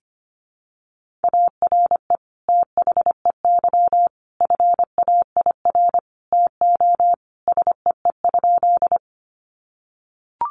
Morse code, heard as a keyed tone, about 25 wpm, 700 Hz. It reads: ARE THEY FAIR TO SEE?
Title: Sentences from Top 1300 Words 25wpm